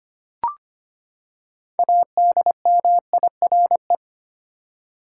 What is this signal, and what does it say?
Morse code, heard as a keyed tone, about 25 wpm, 700 Hz. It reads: ADMIRE
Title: Six Letter Words - 25wpm